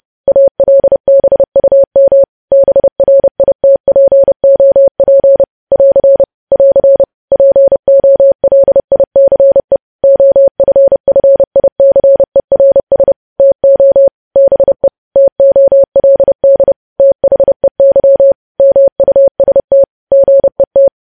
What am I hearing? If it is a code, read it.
Morse code, heard as a keyed tone, about 30 wpm, 550 Hz. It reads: ALBUM BRITPOP <AR> <AR> POLICE OFFICERS TO BE TOLD THEY MUST GET